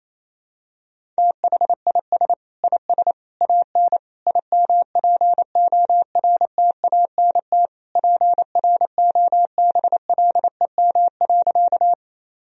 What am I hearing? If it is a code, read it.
Morse code, heard as a keyed tone, about 28 wpm, 700 Hz. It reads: THIS IS AN IMPORTANT PROBLEM.
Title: Sentences from Top 1000 Words - Review Part 2 - 28wpm